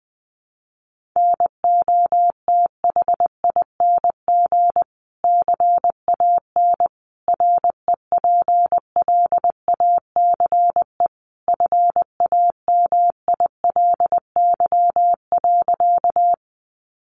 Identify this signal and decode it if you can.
Morse code, heard as a keyed tone, about 20 wpm, 700 Hz. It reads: NOTHING CAN REPLACE FAMILY.
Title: Sentences from Top 1400 Words 20wpm